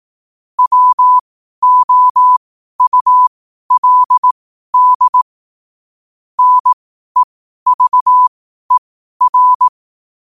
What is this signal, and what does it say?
Morse code, heard as a keyed tone, about 18 wpm, spaced as for 12 wpm, 1000 Hz. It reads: WOULD NEVER
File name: qotdfiles / 12wpm